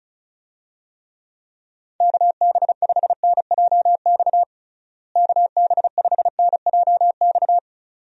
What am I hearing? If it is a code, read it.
Morse code, heard as a keyed tone, about 36 wpm, 700 Hz. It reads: KB5NJX KB5NJX
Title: USA Call Signs - 6 Characters - Repeated 2x - 35wpm